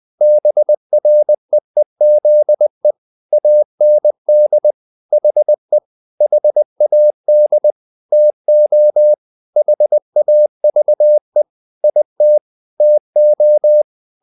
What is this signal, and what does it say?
Morse code, heard as a keyed tone, about 20 wpm, 600 Hz. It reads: BREEZE AND HE HAD TO HAVE IT TO
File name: the old man and the sea part 2 20wpm